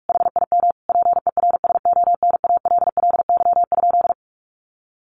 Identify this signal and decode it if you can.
Morse code, heard as a keyed tone, about 44 wpm, 700 Hz. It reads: HIM PERSONALLY?
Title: Sentences from Top 1400 Words 45wpm